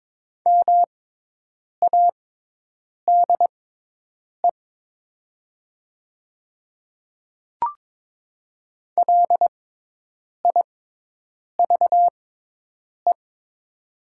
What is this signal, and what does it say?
Morse code, heard as a keyed tone, about 22 wpm, spaced as for 8 wpm, 700 Hz. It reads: MADE LIVE
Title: 6x Character Spacing - Top 200 Words - 22wpm